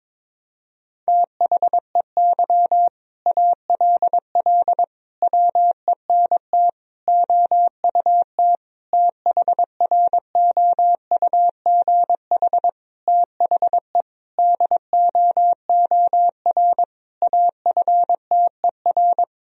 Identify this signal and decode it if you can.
Morse code, heard as a keyed tone, about 22 wpm, 700 Hz. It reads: THEY ALL WENT OUT THROUGH THE DOOR AFTER
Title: Sentences from Top 1000 Words - Review Part 1 - 22wpm